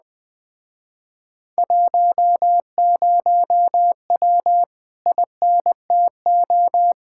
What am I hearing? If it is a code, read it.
Morse code, heard as a keyed tone, about 20 wpm, 700 Hz. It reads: 10W INTO
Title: Speed-Racing Common QSO Phrases - 17wpm